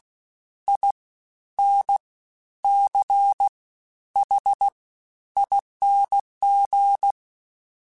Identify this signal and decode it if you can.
Morse code, heard as a keyed tone, about 16 wpm, 800 Hz. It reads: I N C H ING
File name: Special INGs-mp3s / Special INGs-mp3s / Fast ing @15